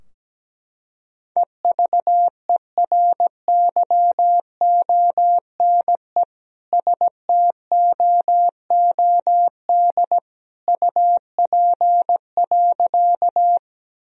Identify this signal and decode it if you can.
Morse code, heard as a keyed tone, about 17 wpm, 700 Hz. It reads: EVERYONE STOOD UP.
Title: Sentences from Top 1000 Words - Review Part 2 - 17wpm